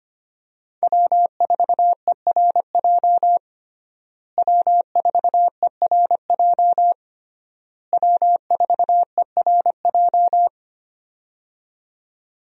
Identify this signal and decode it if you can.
Morse code, heard as a keyed tone, about 25 wpm, 700 Hz. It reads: W4ERJ W4ERJ W4ERJ
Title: USA Call Signs - 5 Characters - Repeated 3x - 25wpm